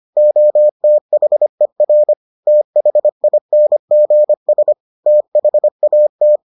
Morse code, heard as a keyed tone, about 25 wpm, 600 Hz. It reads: OTHER THINGS THAT